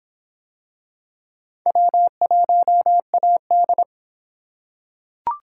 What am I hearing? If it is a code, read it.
Morse code, heard as a keyed tone, about 23 wpm, 700 Hz. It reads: W1AD